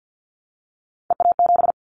Morse code, heard as a keyed tone, about 50 wpm, 700 Hz. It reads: EA7
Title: Callsign Prefixes for Top 25 DXCC - 50wpm